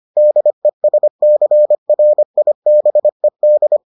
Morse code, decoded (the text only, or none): DESCRIBED